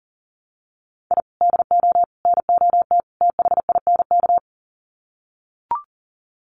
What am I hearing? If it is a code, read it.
Morse code, heard as a keyed tone, about 38 wpm, 700 Hz. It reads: I DO NOT THINK
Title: Sets of 4 Words 40wpm